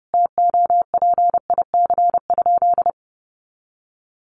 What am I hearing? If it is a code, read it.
Morse code, heard as a keyed tone, about 30 wpm, 700 Hz. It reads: TOPIC?
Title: Sentences from Top 1200 Words 30wpm